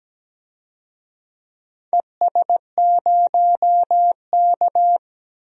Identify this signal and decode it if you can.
Morse code, heard as a keyed tone, about 17 wpm, 700 Hz. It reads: ES0K
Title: DXCC-100 Call Signs - 4 Characters - 17wpm